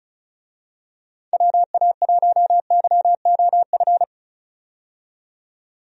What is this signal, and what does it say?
Morse code, heard as a keyed tone, about 36 wpm, 700 Hz. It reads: WA1YOF
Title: USA Call Signs - 6 Characters - 35wpm